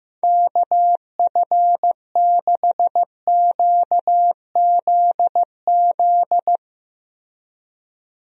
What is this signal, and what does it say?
Morse code, heard as a keyed tone, about 15 wpm, 700 Hz. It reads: KF6QZZ
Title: USA Call Signs - 6 Characters - Repeated 2x - 15wpm